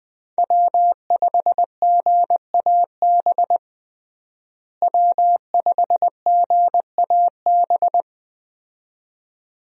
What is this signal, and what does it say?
Morse code, heard as a keyed tone, about 20 wpm, 700 Hz. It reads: W5GAB W5GAB
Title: USA Call Signs - 5 Characters - Repeated 2x - 20wpm